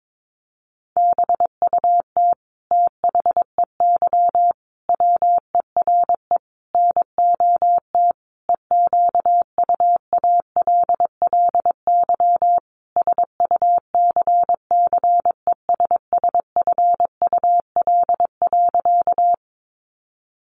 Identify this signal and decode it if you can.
Morse code, heard as a keyed tone, about 22 wpm, 700 Hz. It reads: BUT THEY WERE NOT EQUALLY SUCCESSFUL.
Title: Sentences from Top 1400 Words 22wpm